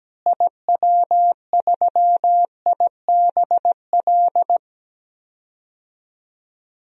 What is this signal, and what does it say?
Morse code, heard as a keyed tone, about 17 wpm, 700 Hz. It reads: IW3IBL
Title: DXCC-100 Call Signs - 6 Characters - Repeated 3x - 17wpm